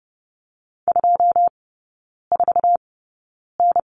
Morse code, heard as a keyed tone, about 30 wpm, spaced as for 9 wpm, 700 Hz. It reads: 24N
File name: course / 30:10wpm / 087 Lesson 15e - Number 5 - Three Characters 30wpm